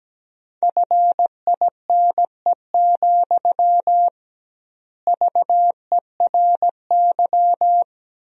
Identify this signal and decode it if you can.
Morse code, heard as a keyed tone, about 17 wpm, 700 Hz. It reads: FINE, VERY
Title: 2x Word-Spacing - Sentences from Top 500 Words - 17wpm